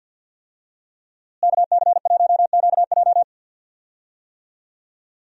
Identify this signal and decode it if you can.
Morse code, heard as a keyed tone, about 47 wpm, 700 Hz. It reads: KC1QJ
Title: USA Call Signs - 5 Characters - Repeated 2x - 50wpm